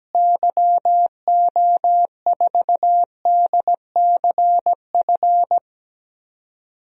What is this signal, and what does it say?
Morse code, heard as a keyed tone, about 17 wpm, 700 Hz. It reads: YO4DCF
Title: DXCC-100 Call Signs - 6 Characters - Repeated 2x - 17wpm